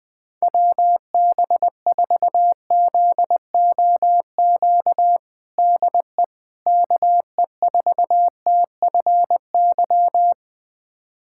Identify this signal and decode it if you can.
Morse code, heard as a keyed tone, about 20 wpm, 700 Hz. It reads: WB4ZOQ DE KE4TFY
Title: USA Callsign DE Callsign 20wpm